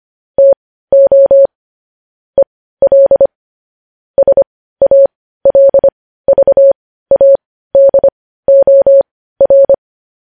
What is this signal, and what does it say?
Morse code, heard as a keyed tone, about 25 wpm, spaced as for 15 wpm, 550 Hz. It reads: TO EL SALVADOR
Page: news-headlines-in-morse-code-at-15-wpm